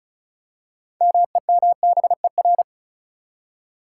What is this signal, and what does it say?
Morse code, heard as a keyed tone, about 34 wpm, 700 Hz. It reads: MEMBER
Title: Six Letter Words - 35wpm